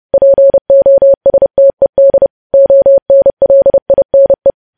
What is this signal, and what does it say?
Morse code, heard as a keyed tone, about 30 wpm, 550 Hz. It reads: POSTED ONLINE
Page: news headlines in morse code at 30 wpm